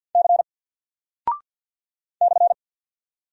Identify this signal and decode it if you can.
Morse code, heard as a keyed tone, about 48 wpm, 700 Hz. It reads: C /
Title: Single Character (Letter:Number:Punctuation:Prosign) 50wpm